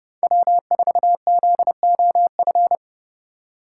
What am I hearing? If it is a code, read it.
Morse code, heard as a keyed tone, about 30 wpm, 700 Hz. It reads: W4ZOF
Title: Code Groups - 5 Letter:Numbers - 30wpm